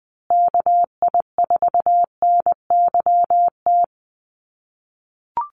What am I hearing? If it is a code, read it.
Morse code, heard as a keyed tone, about 20 wpm, 700 Hz. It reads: KI4NYT
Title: USA Call Signs Repeated 2x 20wpm